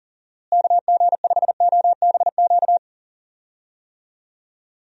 Morse code, heard as a keyed tone, about 37 wpm, 700 Hz. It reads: KG5OBQ
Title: USA Call Signs - 6 Characters - Repeated 2x - 40wpm